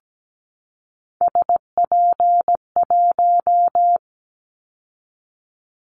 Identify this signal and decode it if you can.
Morse code, heard as a keyed tone, about 17 wpm, 700 Hz. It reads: SP1
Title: Callsign Prefixes for Top 25 DXCC - 17wpm